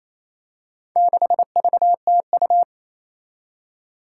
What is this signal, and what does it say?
Morse code, heard as a keyed tone, about 28 wpm, 700 Hz. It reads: 6VTU
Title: Four Challenging Letter-Number Combinations 28wpm